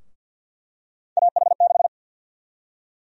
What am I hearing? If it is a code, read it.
Morse code, heard as a keyed tone, about 52 wpm, 700 Hz. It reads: AH6
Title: USA Call Sign Prefixes - Repeated 2x - 50wpm